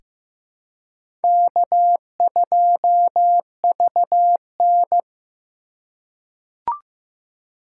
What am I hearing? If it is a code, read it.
Morse code, heard as a keyed tone, about 15 wpm, 700 Hz. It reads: K2VN
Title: USA Call Signs - 4 Characters - 4x Character Spacing - 15wpm